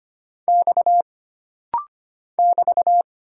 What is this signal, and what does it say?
Morse code, heard as a keyed tone, about 25 wpm, 700 Hz. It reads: X <BT>
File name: Single Character (Letter:Number:Punctuation:Prosign) - Rapid-Fire - 25wpm